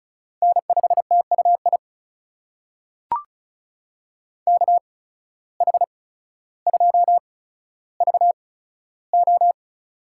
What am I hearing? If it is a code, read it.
Morse code, heard as a keyed tone, about 34 wpm, spaced as for 15 wpm, 700 Hz. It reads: N5TUI KH2VO
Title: USA Call Signs - 5 Characters - 8x Character Spacing - 35wpm